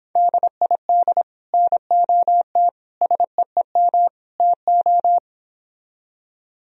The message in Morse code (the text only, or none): DID NOT SEEM TO